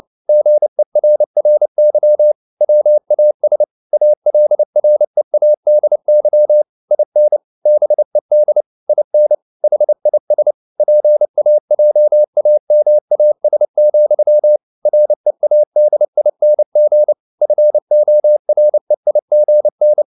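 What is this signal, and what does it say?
Morse code, heard as a keyed tone, about 29 wpm, 600 Hz. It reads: GERRY WAS ALREADY IN BED IN HIS PAJAMAS, READING FOREIGN
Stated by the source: Morse code at 29 wpm, sending Ken Follett's novel Never in English, from CW med Gnister.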